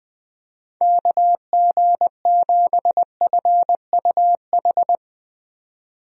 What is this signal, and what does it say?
Morse code, heard as a keyed tone, about 20 wpm, 700 Hz. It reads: KG7FUH
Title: USA Call Signs - 6 Characters - 20wpm